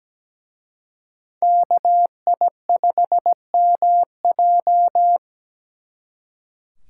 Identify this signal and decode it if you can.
Morse code, heard as a keyed tone, about 20 wpm, 700 Hz. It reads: KI5MJ
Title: Speed Racing - USA 2x2 Call Signs 17wpm